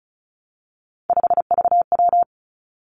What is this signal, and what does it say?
Morse code, heard as a keyed tone, about 36 wpm, 700 Hz. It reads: 5VW
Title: Three Challenging Letter-Number Combinations 35wpm